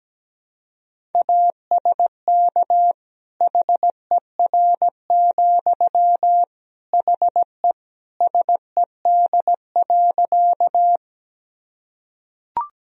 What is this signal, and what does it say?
Morse code, heard as a keyed tone, about 17 wpm, 700 Hz. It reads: ASK HER, HE SED.
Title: Sentences from Top 200 Words and Abbreviations 17wpm